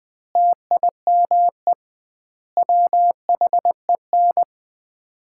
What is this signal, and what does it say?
Morse code, heard as a keyed tone, about 21 wpm, 700 Hz. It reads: TIME WHEN
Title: 2x Word-Spacing - Sets of 5 Words - 20wpm